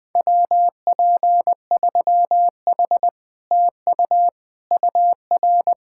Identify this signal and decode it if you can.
Morse code, heard as a keyed tone, about 20 wpm, 700 Hz. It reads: WP3H TU UR
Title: Speed-Racing Common QSO Phrases Encore - 20wpm